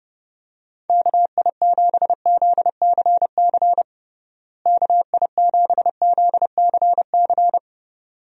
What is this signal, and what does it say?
Morse code, heard as a keyed tone, about 30 wpm, 700 Hz. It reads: KI7ZCC KI7ZCC
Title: USA Call Signs - 6 Characters - Repeated 2x - 30wpm